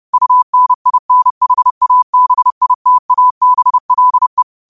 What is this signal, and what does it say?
Morse code, heard as a keyed tone, about 30 wpm, 1000 Hz. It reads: UNINHABITABLE